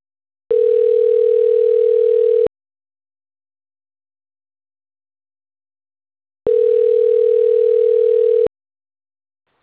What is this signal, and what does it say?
Ringback tone (North America), 440 + 480 Hz.